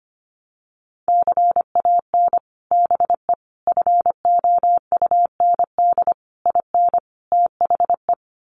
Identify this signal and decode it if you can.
Morse code, heard as a keyed tone, about 25 wpm, 700 Hz. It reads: CAN BE FOUND IN THE
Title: Sets of 5 Words Encore 25wpm